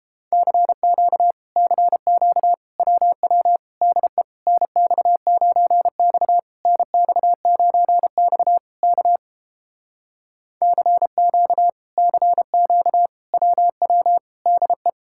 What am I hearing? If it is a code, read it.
Morse code, heard as a keyed tone, about 30 wpm, 700 Hz. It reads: CQ CQ WW DE NX9X NX9X K CQ CQ WW DE